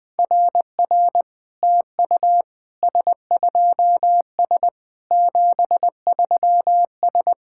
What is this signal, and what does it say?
Morse code, heard as a keyed tone, about 21 wpm, 690 Hz. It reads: RR TU S2S 73S
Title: SOTA QSO Practice 20wpm